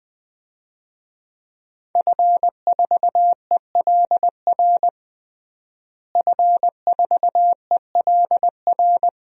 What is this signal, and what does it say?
Morse code, heard as a keyed tone, about 21 wpm, 700 Hz. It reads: F4ELR F4ELR
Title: DXCC-100 Call Signs - 5 Characters - Repeated 3x - 20wpm